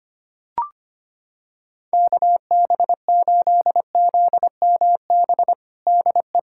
Morse code, heard as a keyed tone, about 25 wpm, 700 Hz. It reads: KB8ZMB DE
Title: USA Callsign DE Callsign - 6 Characters - 25wpm